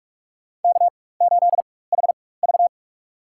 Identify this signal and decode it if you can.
Morse code, heard as a keyed tone, about 46 wpm, spaced as for 21 wpm, 700 Hz. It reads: K8HV
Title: USA Call Signs - 4 Characters - 4x Character Spacing - 45wpm